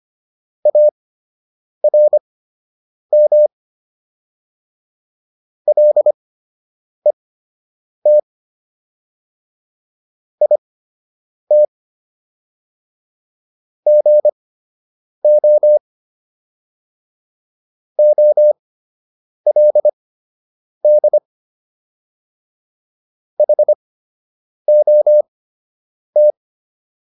Morse code, heard as a keyed tone, about 25 wpm, spaced as for 8 wpm, 600 Hz. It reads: ARM LET IT GO OLD HOT